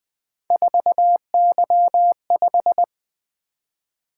Morse code, heard as a keyed tone, about 20 wpm, 700 Hz. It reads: 4Y5
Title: Three Letter-Number Combinations - 20wpm